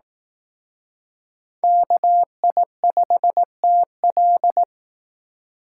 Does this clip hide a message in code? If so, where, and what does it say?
Yes: Morse code from 1.6 to 4.6 s, heard as a keyed tone, about 20 wpm, 700 Hz. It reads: KI5TL